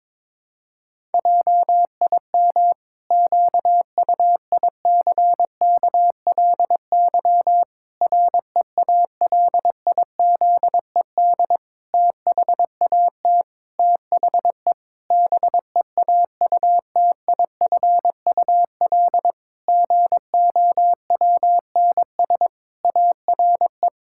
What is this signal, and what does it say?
Morse code, heard as a keyed tone, about 22 wpm, 700 Hz. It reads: JIM QUICKLY REALIZED THAT THE BEAUTIFUL GOWNS ARE
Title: Pangram Challenge v2 - 22wpm